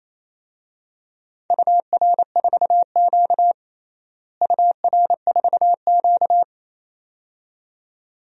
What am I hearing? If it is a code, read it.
Morse code, heard as a keyed tone, about 28 wpm, 700 Hz. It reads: UR4Q UR4Q
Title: DXCC-100 Call Signs - 4 Characters - Repeated 2x - 28wpm